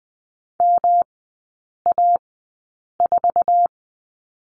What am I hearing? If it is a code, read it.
Morse code, heard as a keyed tone, about 20 wpm, spaced as for 8 wpm, 700 Hz. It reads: MA4
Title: 107 Lesson 18e - Letter m - Three Characters 20wpm